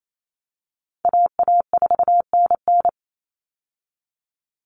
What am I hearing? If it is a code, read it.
Morse code, heard as a keyed tone, about 28 wpm, 700 Hz. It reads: AA4NN
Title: Call Signs Repeated 3x - 28wpm